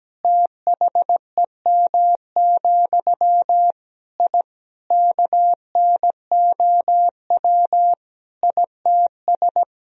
Morse code, heard as a keyed tone, about 17 wpm, 700 Hz. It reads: THEM, I KNOW ITS
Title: Sentences from Top 1000 Words - Review Part 1 - 17wpm